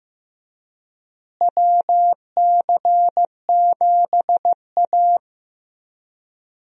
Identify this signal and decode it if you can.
Morse code, heard as a keyed tone, about 15 wpm, 700 Hz. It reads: WC7A
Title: USA Call Signs - 4 Characters - Repeated 2x - 15wpm